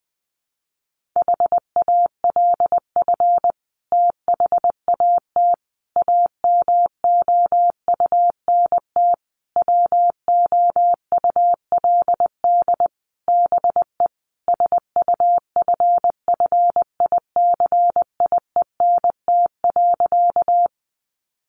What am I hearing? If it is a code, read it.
Morse code, heard as a keyed tone, about 20 wpm, 700 Hz. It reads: HALF THAT AMOUNT WOULD BE SUFFICIENT.
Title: Sentences from Top 1400 Words 20wpm